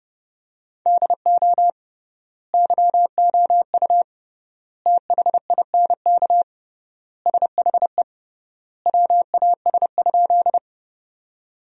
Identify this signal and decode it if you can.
Morse code, heard as a keyed tone, about 32 wpm, 700 Hz. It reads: DO YOU THINK SHE WAS?
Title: 3x Word-Spacing - Sentences from Top 200 Words - 30wpm